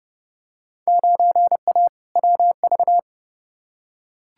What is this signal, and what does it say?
Morse code, heard as a keyed tone, about 30 wpm, 700 Hz. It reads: 9A WV
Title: Field Day Exchanges - 30wpm